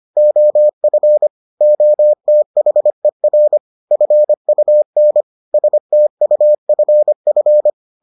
Morse code, heard as a keyed tone, about 25 wpm, 600 Hz. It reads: OF OTHER FUN STUFF